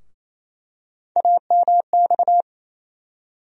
Morse code, heard as a keyed tone, about 28 wpm, 700 Hz. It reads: AMX
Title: USA Call Sign Postfixes - 3 Letters 28wpm